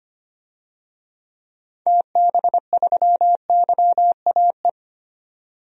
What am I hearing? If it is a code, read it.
Morse code, heard as a keyed tone, about 25 wpm, 700 Hz. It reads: TB3YAE
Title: DXCC-100 Call Signs - 6 Characters - 25wpm